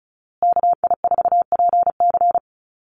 Morse code, heard as a keyed tone, about 34 wpm, 700 Hz. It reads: KI4PC
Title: USA Call Signs Repeated 3x 35wpm